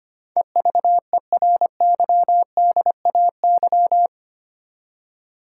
Morse code, heard as a keyed tone, about 25 wpm, 700 Hz. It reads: EVERYDAY
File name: Eight Letter Words - 25wpm